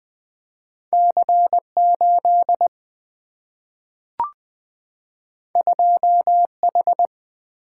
Morse code, heard as a keyed tone, about 20 wpm, 700 Hz. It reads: C8 2H
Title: Single-Letter Single-Number 20wpm